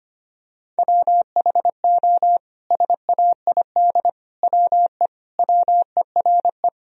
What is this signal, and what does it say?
Morse code, heard as a keyed tone, about 25 wpm, 700 Hz. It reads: WHO SAID WE WERE
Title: Sentences from Top 1000 Words - Review Part 4 - 25wpm